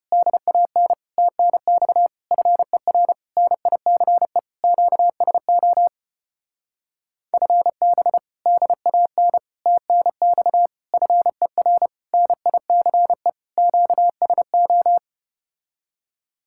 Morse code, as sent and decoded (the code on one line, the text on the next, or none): -.. .- -. / - -. -..- / ..-. .-. / -. .. -.-. / --.- ... --- / ..-. -... / -.. .- -. / - -. -..- / ..-. .-. / -. .. -.-. / --.- ... ---
DAN TNX FER NICE QSO FB DAN TNX FER NICE QSO